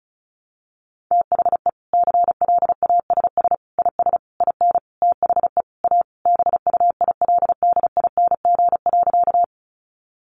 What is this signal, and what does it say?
Morse code, heard as a keyed tone, about 34 wpm, 700 Hz. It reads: THE CLASS IS IN THE A BUILDING.
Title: Additional Easy Phrases 35wpm